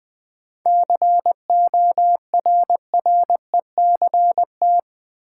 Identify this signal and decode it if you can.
Morse code, heard as a keyed tone, about 20 wpm, 700 Hz. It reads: CORRECT
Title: Sentences from Top 700 Words - New Word Drill - 20wpm